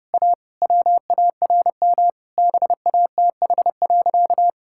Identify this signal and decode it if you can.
Morse code, heard as a keyed tone, about 30 wpm, 700 Hz. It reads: A WARM BATH.
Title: Sentences from Top 1600 Words 30wpm